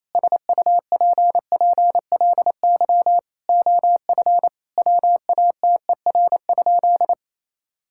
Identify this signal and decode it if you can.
Morse code, heard as a keyed tone, about 28 wpm, 700 Hz. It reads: SUPPLY OF WATER?
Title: Sentences from Top 1000 Words - Review Part 5 - 28wpm